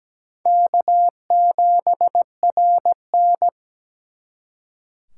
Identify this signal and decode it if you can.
Morse code, heard as a keyed tone, about 20 wpm, 700 Hz. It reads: K7RN
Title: Speed Racing - USA 1x2 Call Signs 17wpm